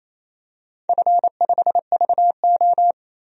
Morse code, heard as a keyed tone, about 28 wpm, 700 Hz. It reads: F5VO
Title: DXCC-100 Call Signs - 4 Characters - 28wpm